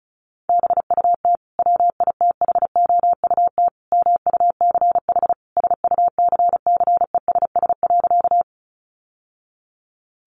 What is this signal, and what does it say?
Morse code, heard as a keyed tone, about 34 wpm, 700 Hz. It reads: BUT WITHOUT MUCH SUCCESS.
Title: Sentences from Top 900 Words 35wpm